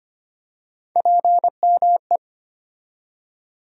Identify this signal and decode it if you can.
Morse code, heard as a keyed tone, about 25 wpm, 700 Hz. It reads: PME
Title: USA Call Sign Postfixes - 3 Letters 25wpm